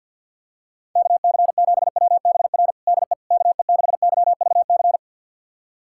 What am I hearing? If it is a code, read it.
Morse code, heard as a keyed tone, about 47 wpm, 700 Hz. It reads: KC7WBR DE KE6YVC
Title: USA Callsign DE Callsign - 6 Characters - 50wpm